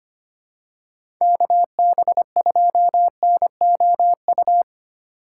Morse code, heard as a keyed tone, about 25 wpm, 700 Hz. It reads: KB2NOU
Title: USA Call Signs - 6 Characters - 4x Character Spacing - 25wpm